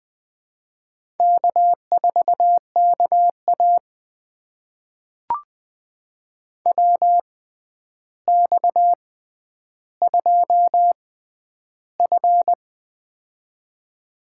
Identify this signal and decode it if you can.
Morse code, heard as a keyed tone, about 20 wpm, spaced as for 10 wpm, 700 Hz. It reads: K4KA WX2F
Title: USA Call Signs - 4 Characters - 6x Character Spacing - 20wpm